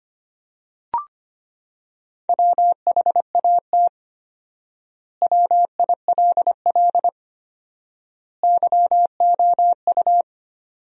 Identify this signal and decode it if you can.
Morse code, heard as a keyed tone, about 25 wpm, 700 Hz. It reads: WHAT WILL YOU
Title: 4x Word-Spacing - Sentences from Top 100 Words - 25wpm